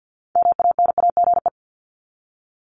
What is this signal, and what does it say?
Morse code, heard as a keyed tone, about 50 wpm, 700 Hz. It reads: MANAGE
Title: Top 1251-1500 Words 50wpm